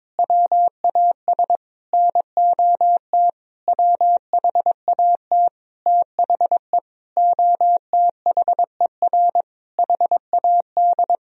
Morse code, heard as a keyed tone, about 22 wpm, 700 Hz. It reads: WAS NOT WHAT THE OTHER HAD
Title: Sentences from Top 1000 Words - Review Part 1 - 22wpm